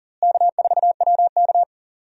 Morse code, heard as a keyed tone, about 40 wpm, 700 Hz. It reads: K4WK
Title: USA Call Signs - 4 Characters - Repeated 2x - 40wpm